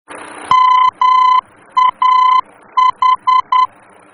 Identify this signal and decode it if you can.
Morse code, heard as a keyed tone, about 10 wpm, 1020 Hz. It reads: MAH
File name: germany / mx / MAH-D-VOR